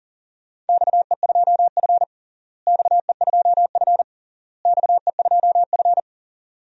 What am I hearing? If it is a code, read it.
Morse code, heard as a keyed tone, about 40 wpm, 700 Hz. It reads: XE2F XE2F XE2F
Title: DXCC-100 Call Signs - 4 Characters - Repeated 3x - 40wpm